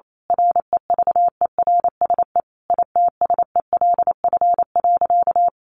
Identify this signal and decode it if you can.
Morse code, heard as a keyed tone, about 28 wpm, 700 Hz. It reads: REVERSE ITSELF.
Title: A Sound Of Thunder 28wpm